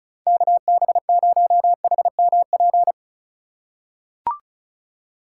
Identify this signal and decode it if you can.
Morse code, heard as a keyed tone, about 34 wpm, 690 Hz. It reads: KB0HMP
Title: USA Call Signs - 6 Characters - 3x Character Spacing - 35wpm